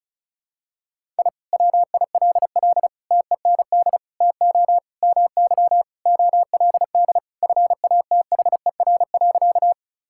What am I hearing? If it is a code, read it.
Morse code, heard as a keyed tone, about 34 wpm, 700 Hz. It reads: I WILL TEND TO MY OLD FATHER.
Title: Sentences from Top 1000 Words - Review Part 5 - 35wpm